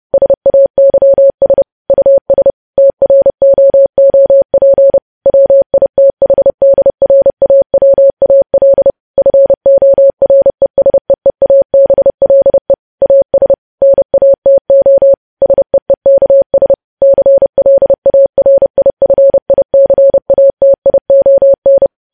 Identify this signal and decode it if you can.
Morse code, heard as a keyed tone, about 30 wpm, 550 Hz. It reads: SAYS US TROOP WITHDRAWAL FORESEEABLE AS NATO SEEKS CLARIFICATION